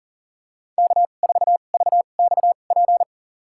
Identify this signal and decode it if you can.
Morse code, heard as a keyed tone, about 40 wpm, spaced as for 29 wpm, 700 Hz. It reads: K4VXP
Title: USA Call Signs - 2x Character Spacing - 40wpm